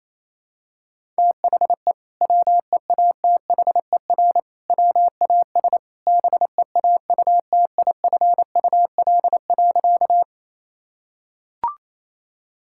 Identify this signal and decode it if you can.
Morse code, heard as a keyed tone, about 28 wpm, 700 Hz. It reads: THE WEATHER WAS BEAUTIFUL.
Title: Sentences from Top 1000 Words - Review Part 5 - 28wpm